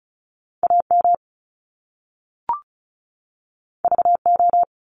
Morse code, heard as a keyed tone, about 36 wpm, 700 Hz. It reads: AM VO
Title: Top Two Letter Combinations in Words 35wpm